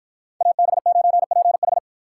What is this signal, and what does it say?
Morse code, heard as a keyed tone, about 52 wpm, 700 Hz. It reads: AB9PH